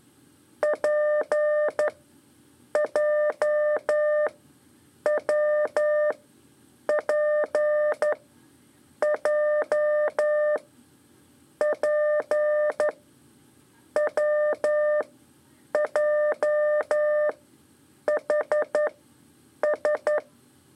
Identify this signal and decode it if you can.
Morse code, heard as a keyed tone, about 11 wpm, spaced as for 7 wpm, 600 Hz. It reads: PJWPJ P WJHS
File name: 3 lezione CW 2 AIR